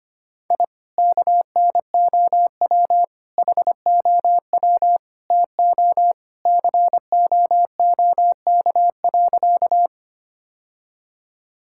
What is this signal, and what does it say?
Morse code, heard as a keyed tone, about 25 wpm, 700 Hz. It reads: I KNOW HOW TO COOK.
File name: practice / 25wpm / Sentences with 3-Word Phrases - 25wpm